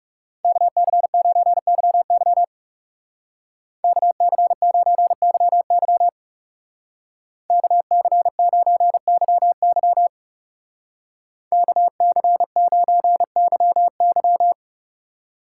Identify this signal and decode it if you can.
Morse code, heard as a keyed tone, about 36 wpm, 700 Hz. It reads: KC9YY KC9YY KC9YY KC9YY
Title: Speed Racing - USA 2x2 Call Signs 30wpm